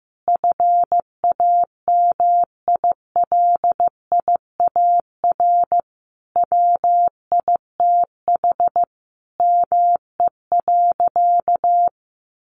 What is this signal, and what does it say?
Morse code, heard as a keyed tone, about 15 wpm, 700 Hz. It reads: FAMILIAR WITH ME.
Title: Sentences from Top 1000 Words 15wpm